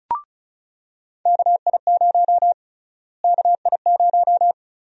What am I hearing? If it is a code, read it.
Morse code, heard as a keyed tone, about 36 wpm, 700 Hz. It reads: KI0 KI0
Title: USA Call Sign Prefixes - Repeated 2x - 35wpm